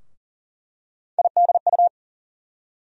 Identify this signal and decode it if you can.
Morse code, heard as a keyed tone, about 40 wpm, 700 Hz. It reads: IDU
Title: USA Call Sign Postfixes - 3 Letters 40wpm